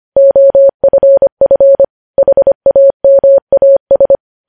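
Morse code, heard as a keyed tone, about 25 wpm, 550 Hz. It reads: OFF HAMAS
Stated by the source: Morse code news headlines sent at 25 wpm.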